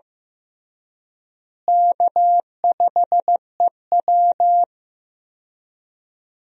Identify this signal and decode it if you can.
Morse code, heard as a keyed tone, about 16 wpm, 700 Hz. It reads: K5EW
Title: USA Call Signs - 4 Characters - Repeated 3x - 15wpm